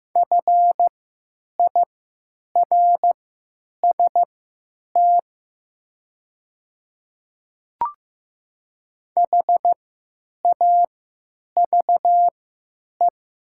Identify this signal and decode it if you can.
Morse code, heard as a keyed tone, about 15 wpm, spaced as for 8 wpm, 700 Hz. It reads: FIRST HAVE
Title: 3x Character Spacing - Top 200 Words - 15wpm